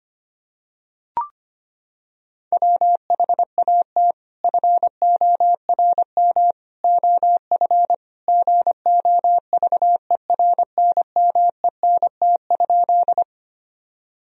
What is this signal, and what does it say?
Morse code, heard as a keyed tone, about 25 wpm, 700 Hz. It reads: WHAT FORM OF GOVERNMENT?
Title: Sentences from Top 1000 Words - Review Part 3 - 25wpm